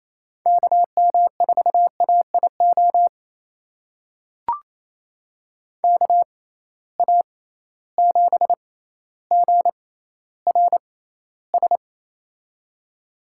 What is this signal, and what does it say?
Morse code, heard as a keyed tone, about 28 wpm, spaced as for 14 wpm, 700 Hz. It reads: KM4AIO KA7GRS
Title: USA Call Signs - 6 Characters - 6x Character Spacing - 28wpm